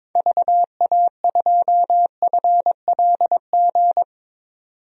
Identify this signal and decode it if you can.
Morse code, heard as a keyed tone, about 22 wpm, 700 Hz. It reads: VA2FLG
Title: DXCC-100 Call Signs - 6 Characters - 22wpm